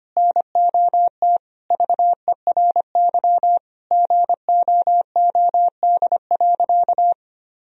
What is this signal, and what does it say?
Morse code, heard as a keyed tone, about 25 wpm, 700 Hz. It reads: NOT VERY GOOD.
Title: Sentences from Top 1000 Words - Review Part 3 - 25wpm